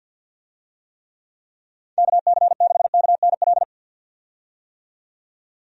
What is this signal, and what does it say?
Morse code, heard as a keyed tone, about 50 wpm, 700 Hz. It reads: KC6KNL
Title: USA Call Signs - 6 Characters - 50wpm